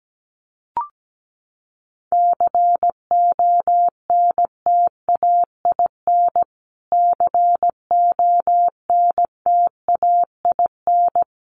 Morse code, heard as a keyed tone, about 17 wpm, 700 Hz. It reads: CONTAIN CONTAIN
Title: Top 500 Words Repeated 17wpm